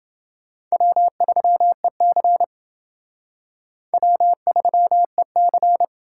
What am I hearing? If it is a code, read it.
Morse code, heard as a keyed tone, about 28 wpm, 700 Hz. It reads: W3EC W3EC